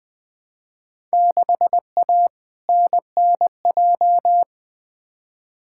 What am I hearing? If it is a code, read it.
Morse code, heard as a keyed tone, about 20 wpm, 700 Hz. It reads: 6A NNJ
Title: Field Day Exchanges Encore - 20wpm